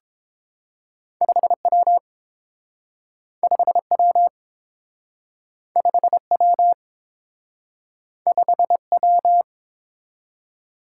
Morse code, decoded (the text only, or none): 5W 5W 5W 5W